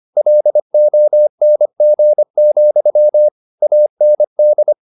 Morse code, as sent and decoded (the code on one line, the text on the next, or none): .-.. --- -. --. --..-- / .- -. -..
LONG, AND